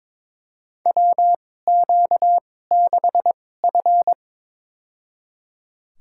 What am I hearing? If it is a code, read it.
Morse code, heard as a keyed tone, about 22 wpm, spaced as for 16 wpm, 700 Hz. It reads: WQ6F
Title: USA Call Signs - 4 Characters - 2x Character Spacing - 22wpm